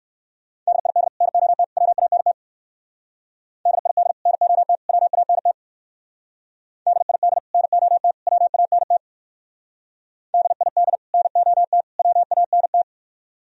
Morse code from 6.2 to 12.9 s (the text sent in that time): DID NOT WANT DID NOT WANT